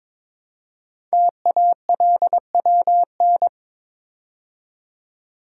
Morse code, heard as a keyed tone, about 22 wpm, 700 Hz. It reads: TALWN
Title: Code Groups - 5 Letters - 22wpm